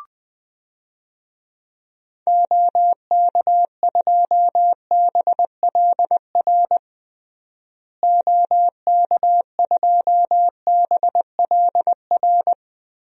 Morse code, heard as a keyed tone, about 20 wpm, 700 Hz. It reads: OK2BLR OK2BLR
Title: DXCC-100 Call Signs - 6 Characters - Repeated 3x - 20wpm